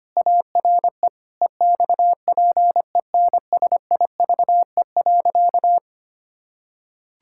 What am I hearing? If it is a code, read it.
Morse code, heard as a keyed tone, about 25 wpm, 700 Hz. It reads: ARE EXPENSIVE.
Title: Pangram Challenge v2 - 25wpm